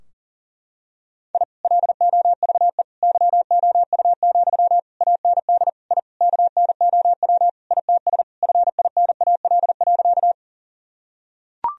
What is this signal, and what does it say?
Morse code, heard as a keyed tone, about 40 wpm, 700 Hz. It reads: I LOVE YOU, AND I KNOW ITS FINAL.
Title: Sentences from Top 1000 Words - Review Part 2 - 40wpm